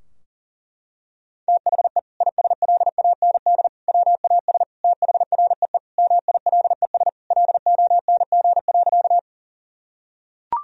Morse code, heard as a keyed tone, about 40 wpm, 700 Hz. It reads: THE ISLAND WAS THREE MILES LONG.
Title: Sentences from Top 1000 Words - Review Part 3 - 40wpm